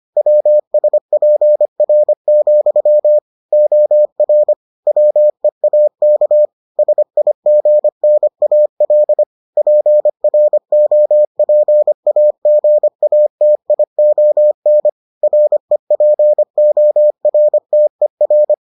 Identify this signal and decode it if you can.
Morse code, heard as a keyed tone, about 25 wpm, 600 Hz. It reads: WSPR, OR WEAK SIGNAL PROPAGATION REPORTER